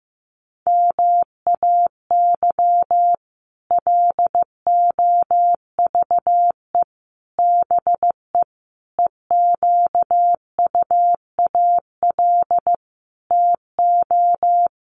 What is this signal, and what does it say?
Morse code, heard as a keyed tone, about 15 wpm, 700 Hz. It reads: MAY LOVE BE EQUAL TO
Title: Sentences from Top 1400 Words 15wpm